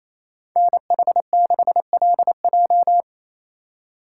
Morse code, heard as a keyed tone, about 34 wpm, 700 Hz. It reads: NH6LJ
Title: Speed Racing - USA 2x2 Call Signs 28wpm